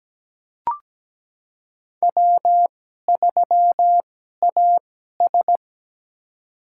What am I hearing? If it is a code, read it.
Morse code, heard as a keyed tone, about 17 wpm, spaced as for 12 wpm, 700 Hz. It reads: W3AS
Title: USA Call Signs - 2x Character Spacing - 17wpm